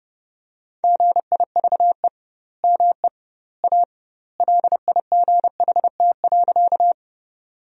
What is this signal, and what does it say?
Morse code, heard as a keyed tone, about 30 wpm, 700 Hz. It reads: GIVE ME A LIGHT.
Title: 2x Word-Spacing - Sentences from Top 200 Words - 30wpm